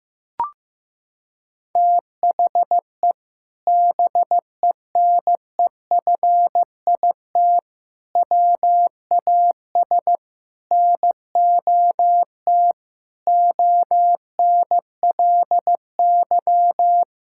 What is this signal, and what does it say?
Morse code, heard as a keyed tone, about 15 wpm, 700 Hz. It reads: THE BENEFIT WAS NOT ONLY
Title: Sentences from Top 1000 Words - Review Part 6 - 15wpm